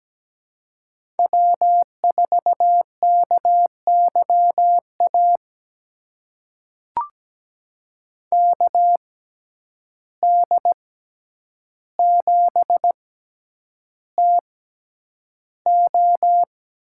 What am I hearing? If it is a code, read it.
Morse code, heard as a keyed tone, about 17 wpm, spaced as for 9 wpm, 700 Hz. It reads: W4KYA KD7TO
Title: USA Call Signs - 6x Character Spacing - 17wpm